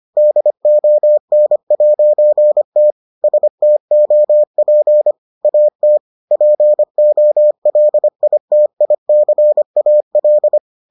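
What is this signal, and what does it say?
Morse code, heard as a keyed tone, about 25 wpm, 600 Hz. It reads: DON'T STOP AT POLITICAL
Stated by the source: Morse edition of the Foundations of Amateur Radio episodes.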